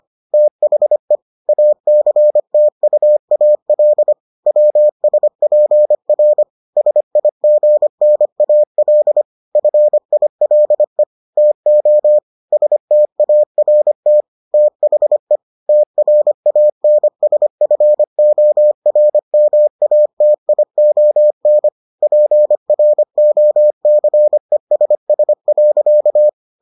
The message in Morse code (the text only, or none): THE ACTUAL WSPR SIGNAL FILE TO START THE TRANSFORMATION PROCESS.